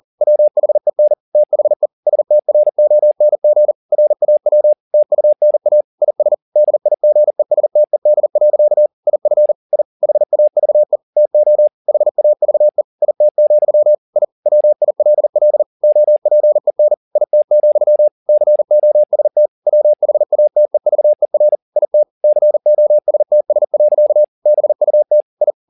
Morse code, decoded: WHEN THE STRONG RAW TUNA IS DIGESTED. IF I HAVE TO HAVE IT, I WILL OPEN IT, COST WHATEVER IT COSTS. BUT I

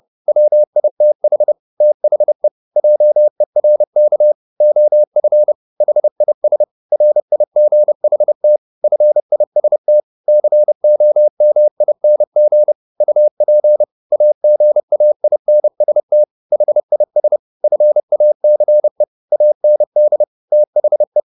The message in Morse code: WITH THE JERK OF HIS RIGHT FIST COMING UP AGAINST HIS FACE AND THE

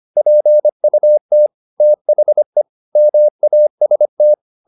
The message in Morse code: PUT THE MAST

